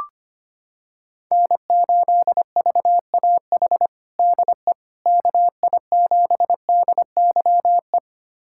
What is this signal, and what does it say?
Morse code, heard as a keyed tone, about 25 wpm, 700 Hz. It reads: N8VAH DE KI7DYE
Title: USA Callsign DE Callsign 25wpm